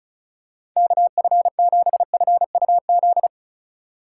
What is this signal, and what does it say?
Morse code, heard as a keyed tone, about 36 wpm, 700 Hz. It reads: KF7FUZ
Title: USA Call Signs - 6 Characters - 4x Character Spacing - 35wpm